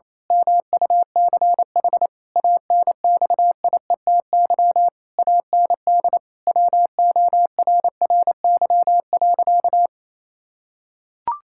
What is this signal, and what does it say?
Morse code, heard as a keyed tone, about 28 wpm, 700 Hz. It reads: MUCH ANXIETY AND WORRY.
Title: Pangram Challenge v2 - 28wpm